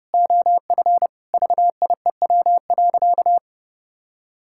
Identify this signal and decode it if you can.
Morse code, heard as a keyed tone, about 32 wpm, 700 Hz. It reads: OF VIEW.
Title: Sentences from Top 1000 Words - Review Part 6 - 30wpm